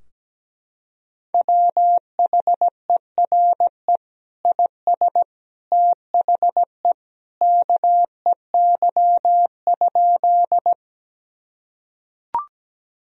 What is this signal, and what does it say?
Morse code, heard as a keyed tone, about 17 wpm, 700 Hz. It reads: WHERE IS THE KEY?
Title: Sentences from Top 1000 Words - Review Part 4 - 17wpm